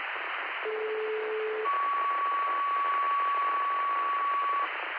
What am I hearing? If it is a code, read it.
Two-tone page, heard as a one-second tone, then a three-second tone, about 430 then 1130 Hz, with noise. It is heard from 0.6 to 4.6 s.